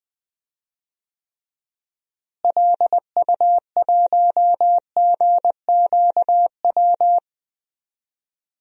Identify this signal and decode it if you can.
Morse code, heard as a keyed tone, about 20 wpm, 690 Hz. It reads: LU1GQW
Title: DXCC-100 Call Signs - 6 Characters - Repeated 2x - 20wpm